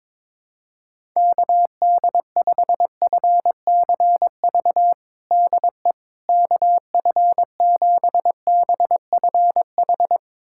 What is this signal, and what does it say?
Morse code, heard as a keyed tone, about 22 wpm, 700 Hz. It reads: KD5FCV DE KF7BFH
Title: USA Callsign DE Callsign 22wpm